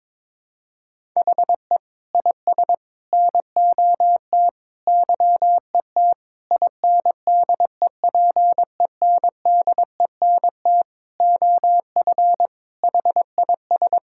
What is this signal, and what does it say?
Morse code, heard as a keyed tone, about 21 wpm, 700 Hz. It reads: HE IS NOT YET INDEPENDENT OF HIS
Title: Sentences from Top 1000 Words - Review Part 6 - 22wpm